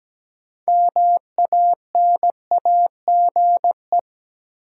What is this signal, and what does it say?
Morse code, heard as a keyed tone, about 17 wpm, 700 Hz. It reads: MANAGE
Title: Sentences from Top 1400 Words - New Word Drill - 17wpm